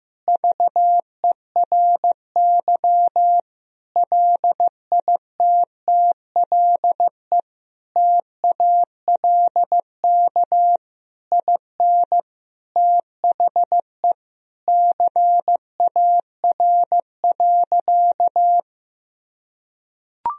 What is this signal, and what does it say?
Morse code, heard as a keyed tone, about 16 wpm, 700 Hz. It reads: VERY LITTLE TALK IN THE CAR.
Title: Sentences from Top 1000 Words - Review Part 1 - 15wpm